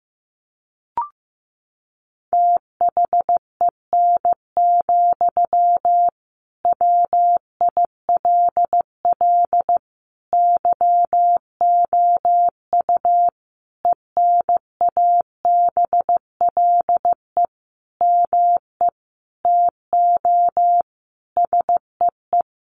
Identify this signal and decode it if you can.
Morse code, heard as a keyed tone, about 15 wpm, 700 Hz. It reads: THEN, WILL YOU ENABLE ME TO SEE
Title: Sentences from Top 1500 Words 15wpm